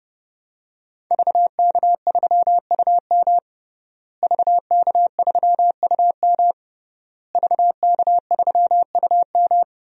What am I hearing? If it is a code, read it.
Morse code, heard as a keyed tone, about 30 wpm, 700 Hz. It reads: VK3UM VK3UM VK3UM
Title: DXCC-100 Call Signs - 5 Characters - Repeated 3x - 30wpm